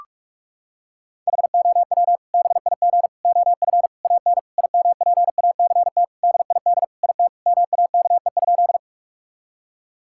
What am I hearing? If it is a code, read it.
Morse code, heard as a keyed tone, about 41 wpm, 700 Hz. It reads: HOW BIG OF AN IMPACT DID IT MAKE?
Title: Sentences from Top 1000 Words - Review Part 6 - 45wpm